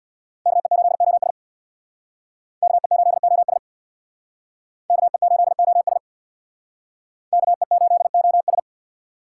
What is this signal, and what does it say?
Morse code, heard as a keyed tone, about 64 wpm, 700 Hz. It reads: KE8OS KE8OS KE8OS KE8OS